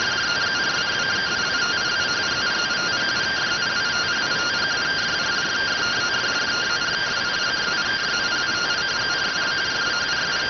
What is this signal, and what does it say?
Two-tone teleprinter signal, about 170 Hz shift.